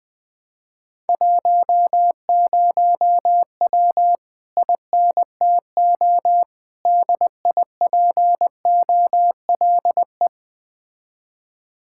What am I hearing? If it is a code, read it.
Morse code, heard as a keyed tone, about 20 wpm, 700 Hz. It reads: 10W INTO DIPOLE
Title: Speed-Racing Common QSO Phrases Encore - 17wpm